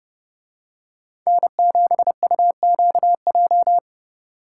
Morse code, heard as a keyed tone, about 30 wpm, 700 Hz. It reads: N7UQJ